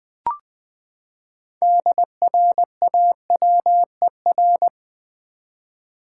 Morse code, heard as a keyed tone, about 20 wpm, 700 Hz. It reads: DRAWER
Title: Six Letter Words - 20wpm